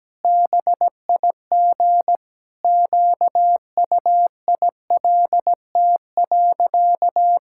Morse code, heard as a keyed tone, about 17 wpm, 700 Hz. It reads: BIG QUILT.
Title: Pangram Challenge v2 - 17wpm